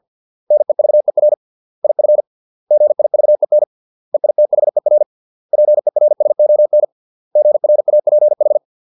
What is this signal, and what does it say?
Morse code, heard as a keyed tone, about 50 wpm, 600 Hz. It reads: NEVER IF GIVEN EITHER PERSON GRAPH
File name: 632 words / 632 Words 50WPM